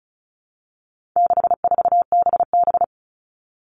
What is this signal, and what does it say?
Morse code, heard as a keyed tone, about 34 wpm, 700 Hz. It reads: 64BB
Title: All Cut Numbers - 4-Digit 35wpm